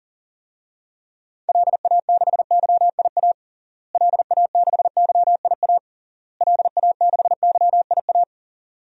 Morse code, heard as a keyed tone, about 40 wpm, 700 Hz. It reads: LA6YIA LA6YIA LA6YIA
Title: DXCC-100 Call Signs - 6 Characters - Repeated 3x - 40wpm